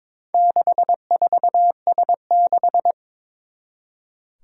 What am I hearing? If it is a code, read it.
Morse code, heard as a keyed tone, about 22 wpm, 700 Hz. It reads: 64S6
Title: Four Challenging Letter-Number Combinations 22wpm